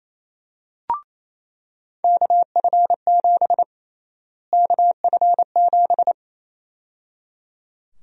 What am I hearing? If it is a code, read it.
Morse code, heard as a keyed tone, about 28 wpm, 700 Hz. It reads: KF7 KF7
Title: USA Call Sign Prefixes - Repeated 2x - 28wpm